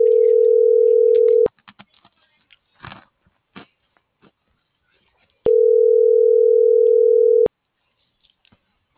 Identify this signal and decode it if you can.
Ringback tone (North America), 440 + 480 Hz.